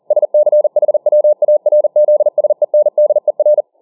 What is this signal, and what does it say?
Morse code, heard as a keyed tone, about 40 wpm, 600 Hz. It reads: SCHWARZSENDER